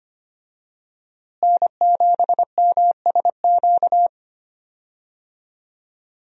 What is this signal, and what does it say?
Morse code, heard as a keyed tone, about 25 wpm, 700 Hz. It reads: N7MSQ
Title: USA Call Signs - 5 Characters - Repeated 3x - 25wpm